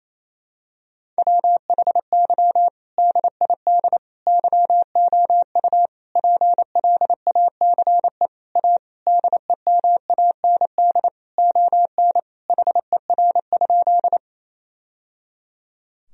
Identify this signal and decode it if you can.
Morse code, heard as a keyed tone, about 28 wpm, 700 Hz. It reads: WHY DID YOU PLACE A DEMAND ON HER?
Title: Sentences from Top 1000 Words - Review Part 4 - 28wpm